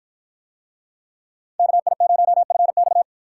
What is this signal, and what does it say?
Morse code, heard as a keyed tone, about 48 wpm, 700 Hz. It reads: KI0FX